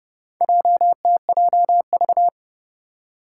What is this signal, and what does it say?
Morse code, heard as a keyed tone, about 30 wpm, 700 Hz. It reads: JTJV
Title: Four Challenging Letter-Number Combinations 30wpm